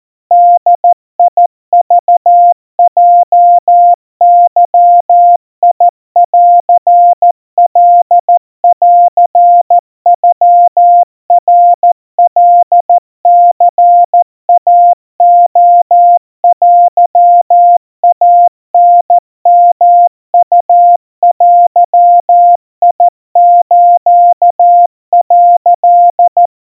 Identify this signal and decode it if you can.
Morse code (Japanese Wabun code), heard as a keyed tone, about 14 wpm, 700 Hz. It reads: ボクヲゲンカンノナカニイレテイタヨウデス」
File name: kawa / Morse sample / kamyu 50 cpm 700 Hz Crispy